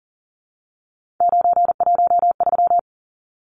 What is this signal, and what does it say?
Morse code, heard as a keyed tone, about 40 wpm, 700 Hz. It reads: 913
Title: Numbers - 3-Digit 40wpm